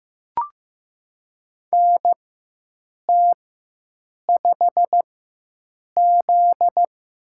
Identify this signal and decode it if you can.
Morse code, heard as a keyed tone, about 16 wpm, spaced as for 7 wpm, 700 Hz. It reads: NT5Z
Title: USA Call Signs - 4x Character Spacing - 15wpm